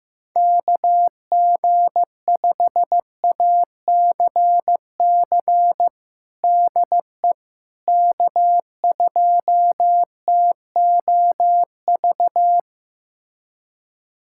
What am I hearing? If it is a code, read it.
Morse code, heard as a keyed tone, about 15 wpm, 700 Hz. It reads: KG5ACC DE K2TOV
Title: USA Callsign DE Callsign 15wpm